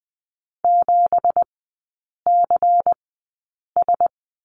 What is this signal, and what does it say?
Morse code, heard as a keyed tone, about 20 wpm, spaced as for 8 wpm, 700 Hz. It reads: 7CS
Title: 198 Lesson 32e - Letter k - Three Characters 20wpm